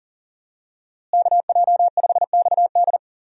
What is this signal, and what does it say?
Morse code, heard as a keyed tone, about 40 wpm, 700 Hz. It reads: KJ5XD
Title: USA Call Signs - 5 Characters - Repeated 2x - 40wpm